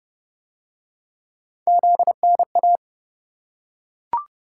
Morse code, heard as a keyed tone, about 28 wpm, 700 Hz. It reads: ZNA